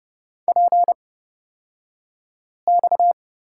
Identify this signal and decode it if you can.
Morse code, heard as a keyed tone, about 30 wpm, spaced as for 5 wpm, 700 Hz. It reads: PX
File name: Lesson 37d - Letter x - Two Characters 30wpm